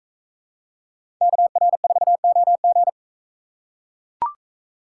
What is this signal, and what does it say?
Morse code, heard as a keyed tone, about 41 wpm, 700 Hz. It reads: KR4OG